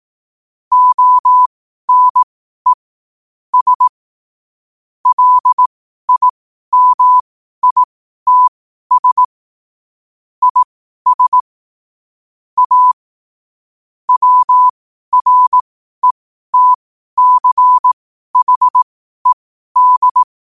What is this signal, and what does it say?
Morse code, heard as a keyed tone, about 18 wpm, spaced as for 12 wpm, 1000 Hz. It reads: ONE S LIMITS IS A WRETCHED
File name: qotdfiles / 12wpm